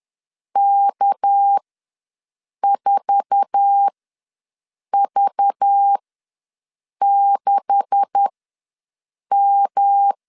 Morse code, heard as a keyed tone, about 10 wpm, spaced as for 6 wpm, 800 Hz. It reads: K4V6M
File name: LEZIONI / 9 286 ws 10wpm 4032